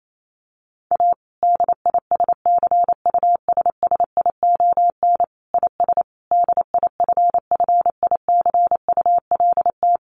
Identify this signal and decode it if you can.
Morse code, heard as a keyed tone, about 28 wpm, 700 Hz. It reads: A DISCUSSION IS DIFFICULT
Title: Sentences from Top 1100 Words 28wpm